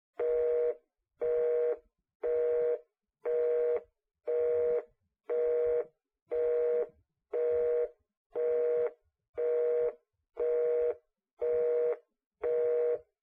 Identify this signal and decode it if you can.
Busy tone (North America), 480 + 620 Hz.
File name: Kitchen Telephone Line Busy Tone